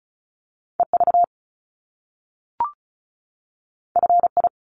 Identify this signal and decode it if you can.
Morse code, heard as a keyed tone, about 36 wpm, 700 Hz. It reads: EV FI
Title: Top Two Letter Combinations in Words 35wpm